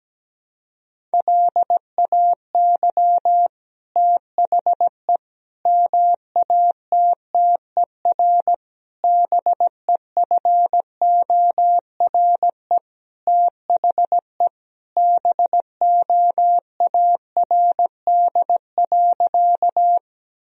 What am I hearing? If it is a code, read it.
Morse code, heard as a keyed tone, about 17 wpm, 700 Hz. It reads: LAY THE MATTER BEFORE THE BOARD.
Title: Sentences from Top 1000 Words - Review Part 4 - 17wpm